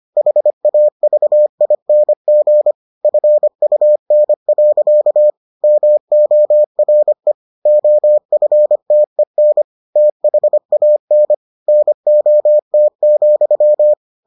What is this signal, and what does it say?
Morse code, heard as a keyed tone, about 25 wpm, 600 Hz. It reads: HAVING FUN. MORE OFTEN THAN NOT,